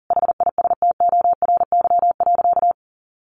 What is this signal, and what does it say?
Morse code, heard as a keyed tone, about 40 wpm, 700 Hz. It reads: HISTORY.
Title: A Sound Of Thunder 40wpm